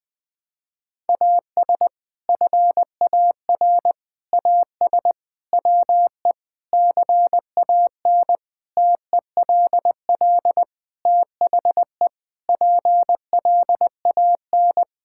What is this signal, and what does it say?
Morse code, heard as a keyed tone, about 20 wpm, 700 Hz. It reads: AS FAR AS WE CAN TELL THE PLAN
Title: Sentences with 6-Word Phrases - 20wpm